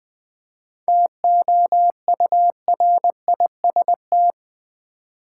Morse code, heard as a keyed tone, about 20 wpm, 700 Hz. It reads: TOURIST
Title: 7 Letter Words - First 3 Letters Given - 20wpm